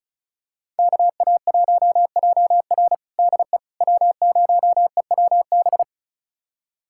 Morse code, heard as a keyed tone, about 34 wpm, 690 Hz. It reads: KA1JR DE W0EWB
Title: USA Callsign DE Callsign - 5 Characters - 35wpm